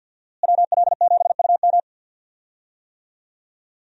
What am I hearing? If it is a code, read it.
Morse code, heard as a keyed tone, about 51 wpm, 700 Hz. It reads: WL7UM
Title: USA Call Signs - 5 Characters - Repeated 3x - 50wpm